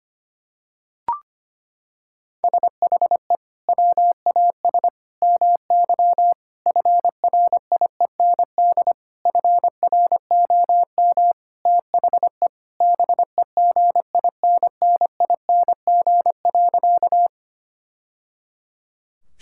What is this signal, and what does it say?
Morse code, heard as a keyed tone, about 25 wpm, 700 Hz. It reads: SHE WAS MY FRIEND FROM THE BEGINNING.
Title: Sentences from Top 1000 Words - Review Part 5 - 25wpm